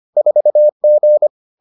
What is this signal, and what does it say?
Morse code, heard as a keyed tone, about 25 wpm, 600 Hz. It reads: 4G